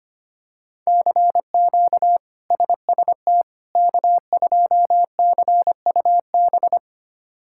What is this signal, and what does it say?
Morse code, heard as a keyed tone, about 25 wpm, 700 Hz. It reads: CQ SST K2CUB
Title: K1USN SST QSO Practice - 25wpm